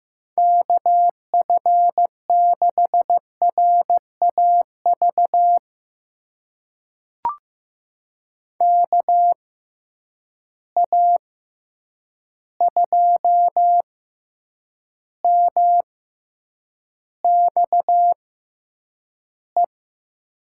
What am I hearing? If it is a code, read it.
Morse code, heard as a keyed tone, about 15 wpm, spaced as for 8 wpm, 700 Hz. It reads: KF6RAV KA2MXE
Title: USA Call Signs - 6x Character Spacing - 15wpm